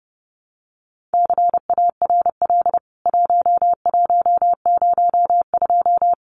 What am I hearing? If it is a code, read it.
Morse code, heard as a keyed tone, about 30 wpm, 700 Hz. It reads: CARL 1102
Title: CWT Exchange 30wpm